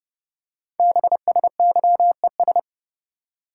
Morse code, heard as a keyed tone, about 28 wpm, 700 Hz. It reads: BSYES
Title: Code Groups - 5 Letters - 30wpm